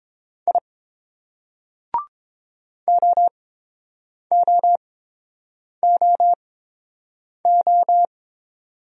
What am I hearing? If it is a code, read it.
Morse code, heard as a keyed tone, about 28 wpm, 690 Hz. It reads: I O O O O